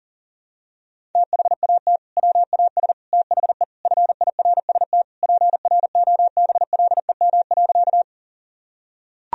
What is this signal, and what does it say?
Morse code, heard as a keyed tone, about 40 wpm, 700 Hz. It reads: THAT WAS THE FIRST PROBLEM.
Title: Sentences from Top 1000 Words - Review Part 2 - 40wpm